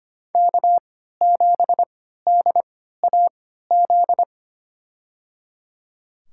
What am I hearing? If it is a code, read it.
Morse code, heard as a keyed tone, about 25 wpm, spaced as for 14 wpm, 700 Hz. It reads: K7DAZ
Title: USA Call Signs - 5 Characters - 3x Character Spacing - 25wpm